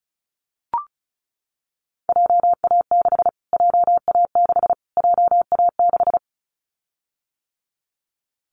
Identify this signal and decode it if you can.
Morse code, heard as a keyed tone, about 34 wpm, 700 Hz. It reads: JA6 JA6 JA6
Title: Callsign Prefixes for Top 25 DXCC - Repeated 3x - 35wpm